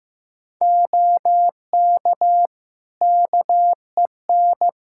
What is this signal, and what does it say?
Morse code, heard as a keyed tone, about 15 wpm, 700 Hz. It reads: OK KEN
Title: Practice QSO - Exchange 3 - More Info - 15wpm